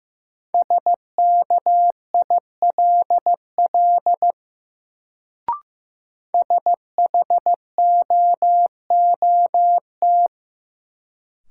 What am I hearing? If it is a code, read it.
Morse code, heard as a keyed tone, about 15 wpm, 700 Hz. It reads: SKILL SHOOT
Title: Sentences from Top 1200 Words - New Word Drill - 15wpm